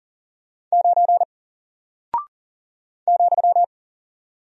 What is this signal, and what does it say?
Morse code, heard as a keyed tone, about 40 wpm, 700 Hz. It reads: 9 ,
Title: Single Character (Letter:Number:Punctuation:Prosign) 40wpm